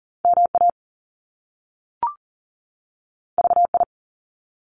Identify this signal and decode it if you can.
Morse code, heard as a keyed tone, about 40 wpm, 700 Hz. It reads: MA VI
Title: Top Two Letter Combinations in Words 40wpm